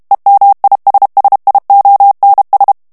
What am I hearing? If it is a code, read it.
Morse code, heard as a keyed tone, about 32 wpm, 800 Hz. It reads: EMISSIONS
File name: morse mp3 / ANGLAIS texte clair 3 vitesse 30 mots minute